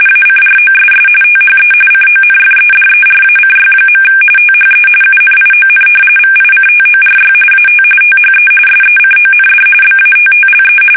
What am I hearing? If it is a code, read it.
RTTY, about 45.45 baud. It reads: ACROSS THE CONFEDERACY. THE UNION VICTORY AT THE SECOND BATTLE
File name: fsk850 / usnavy3